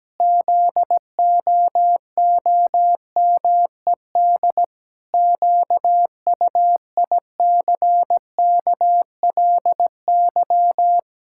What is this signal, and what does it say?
Morse code, heard as a keyed tone, about 17 wpm, 700 Hz. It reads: ZOOMED QUICKLY
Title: Pangram Challenge v2 - 17wpm